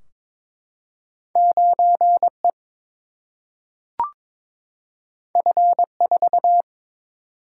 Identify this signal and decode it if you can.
Morse code, heard as a keyed tone, about 22 wpm, 700 Hz. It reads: 9E F4
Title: Single-Letter Single-Number 22wpm